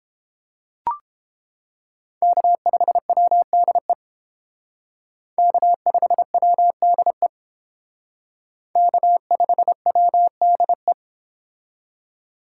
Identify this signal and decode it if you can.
Morse code, heard as a keyed tone, about 28 wpm, 700 Hz. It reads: K5WDE K5WDE K5WDE